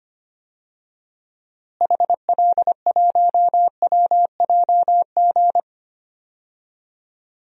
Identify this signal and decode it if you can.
Morse code, heard as a keyed tone, about 25 wpm, 700 Hz. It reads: HL1WJG
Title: DXCC-100 Call Signs - 6 Characters - Repeated 2x - 25wpm